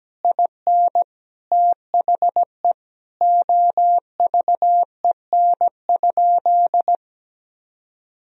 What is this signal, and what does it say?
Morse code, heard as a keyed tone, about 17 wpm, 700 Hz. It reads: IN THE OVEN?
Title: Sentences from Top 1000 Words - Review Part 4 - 17wpm